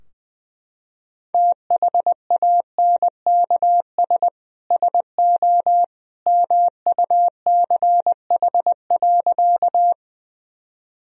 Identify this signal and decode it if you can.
Morse code, heard as a keyed tone, about 20 wpm, 700 Hz. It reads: THANKS SO MUCH.
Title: Sentences from Top 1000 Words - Review Part 3 - 20wpm